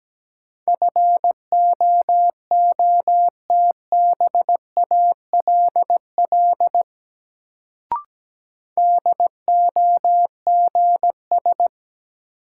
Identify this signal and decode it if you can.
Morse code, heard as a keyed tone, about 17 wpm, 700 Hz. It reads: FOOTBALL DOGS
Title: Sentences from Top 1600 Words - New Word Drill - 17wpm